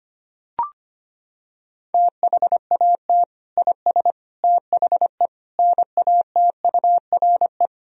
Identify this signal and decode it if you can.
Morse code, heard as a keyed tone, about 25 wpm, 700 Hz. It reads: THAT IS THE NATURE
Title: Sentences from Top 1000 Words - Review Part 4 - 25wpm